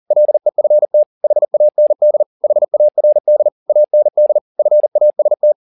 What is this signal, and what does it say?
Morse code, heard as a keyed tone, about 40 wpm, 600 Hz. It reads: LEFT HAND HARD AND FAST